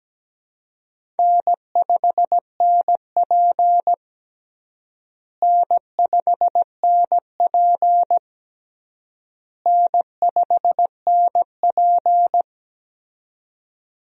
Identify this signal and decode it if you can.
Morse code, heard as a keyed tone, about 17 wpm, 700 Hz. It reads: N5NP N5NP N5NP
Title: USA Call Signs - 4 Characters - Repeated 3x - 17wpm